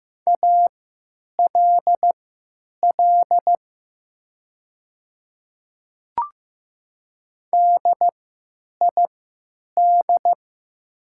Morse code, heard as a keyed tone, about 15 wpm, spaced as for 9 wpm, 700 Hz. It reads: ALL DID
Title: 3x Character Spacing - Top 100 Words - 15wpm